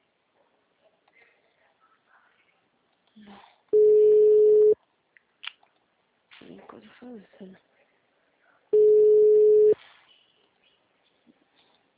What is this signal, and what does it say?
Ringback tone (Europe, 425 Hz), with speech.
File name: out- -2020- - - .1040